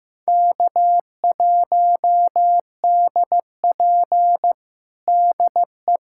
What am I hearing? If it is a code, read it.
Morse code, heard as a keyed tone, about 15 wpm, 700 Hz. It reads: K1DP DE
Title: USA Callsign DE Callsign - 4 Characters - 15wpm